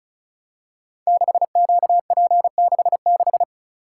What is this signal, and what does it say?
Morse code, heard as a keyed tone, about 36 wpm, 690 Hz. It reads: 6QP66
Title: Code Groups - 5 Letter:Numbers - 35wpm